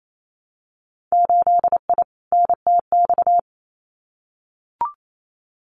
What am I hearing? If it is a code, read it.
Morse code, heard as a keyed tone, about 28 wpm, 690 Hz. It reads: 8I NTX
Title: Winter Field Day Exchanges Encore 28wpm